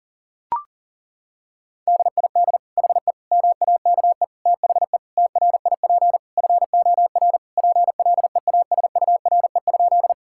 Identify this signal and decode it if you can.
Morse code, heard as a keyed tone, about 40 wpm, 700 Hz. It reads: DID HE MAKE THE TRIP FOR PLEASURE?
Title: Sentences from Top 1600 Words 40wpm